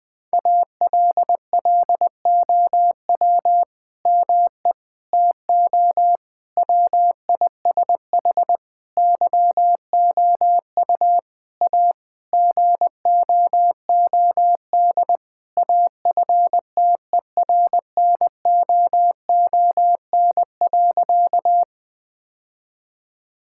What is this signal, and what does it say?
Morse code, heard as a keyed tone, about 20 wpm, 700 Hz. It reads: ALLOW ME TO WISH YOU A GOOD AFTERNOON.
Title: Sentences from Top 1600 Words 20wpm